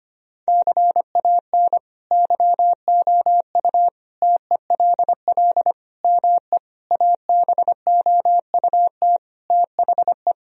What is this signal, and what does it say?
Morse code, heard as a keyed tone, about 25 wpm, 700 Hz. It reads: CAN YOU TELL ME ABOUT THE
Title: Sentences from Top 1000 Words - Review Part 3 - 25wpm